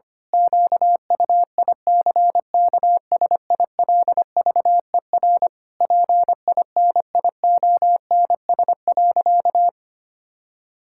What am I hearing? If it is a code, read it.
Morse code, heard as a keyed tone, about 25 wpm, 700 Hz. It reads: QUICKSILVER PINIONS.
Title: Pangram Challenge v2 - 25wpm